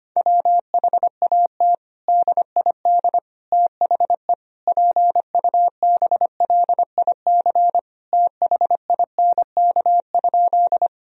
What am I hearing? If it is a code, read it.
Morse code, heard as a keyed tone, about 25 wpm, 700 Hz. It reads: WHAT DID THE PUBLIC THINK?
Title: Sentences from Top 1000 Words - Review Part 3 - 25wpm